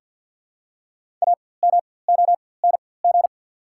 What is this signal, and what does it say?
Morse code, heard as a keyed tone, about 51 wpm, spaced as for 23 wpm, 700 Hz. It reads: AMONG
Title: 4x Character Spacing - Top 500 Words - 50wpm